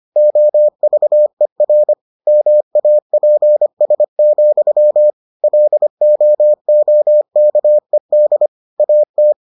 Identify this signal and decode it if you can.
Morse code, heard as a keyed tone, about 25 wpm, 600 Hz. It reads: OVER MAPS, LOOKED AT